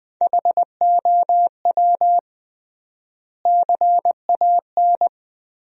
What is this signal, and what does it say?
Morse code, heard as a keyed tone, about 20 wpm, 700 Hz. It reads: HOW CAN
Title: 3x Word-Spacing - Sentences from Top 200 Words - 20wpm